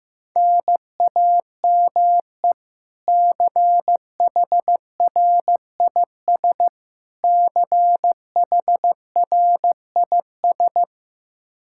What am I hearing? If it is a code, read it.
Morse code, heard as a keyed tone, about 15 wpm, 700 Hz. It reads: NAME CHRIS CHRIS
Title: Practice QSO - Exchange 1 - Beginning - 15wpm